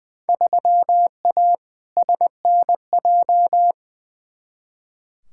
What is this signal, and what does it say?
Morse code, heard as a keyed tone, about 20 wpm, 700 Hz. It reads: 3A SNJ
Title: Field Day Exchanges - 20wpm